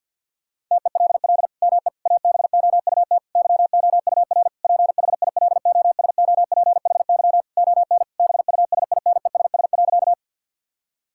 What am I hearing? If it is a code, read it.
Morse code, heard as a keyed tone, about 50 wpm, 700 Hz. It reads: TELL ME ABOUT YOUR PHILOSOPHY ON BUSINESS.
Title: Sentences from Top 1000 Words - Review Part 6 - 50wpm